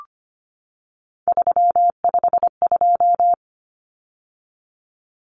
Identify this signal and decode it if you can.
Morse code, heard as a keyed tone, about 25 wpm, 700 Hz. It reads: 352
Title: Numbers - 3-Digit 25wpm